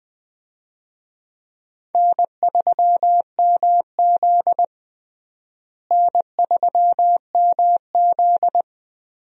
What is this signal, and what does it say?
Morse code, heard as a keyed tone, about 21 wpm, 700 Hz. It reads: N3MZ N3MZ
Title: USA Call Signs - 4 Characters - Repeated 2x - 20wpm